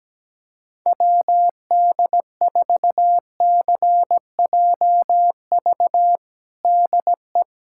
Morse code, heard as a keyed tone, about 17 wpm, 700 Hz. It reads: WD4CJV DE
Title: USA Callsign DE Callsign - 6 Characters - 17wpm